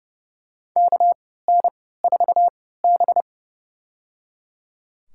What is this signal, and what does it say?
Morse code, heard as a keyed tone, about 30 wpm, spaced as for 17 wpm, 700 Hz. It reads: KN4B
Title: USA Call Signs - 3x Character Spacing - 30wpm